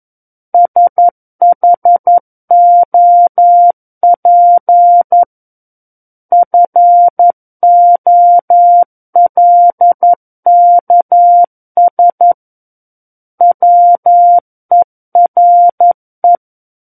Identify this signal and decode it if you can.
Morse code, heard as a keyed tone, about 11 wpm, 700 Hz. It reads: SHOP FOLKS WERE
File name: Book One / 02-Chapter 2 - 11 WPM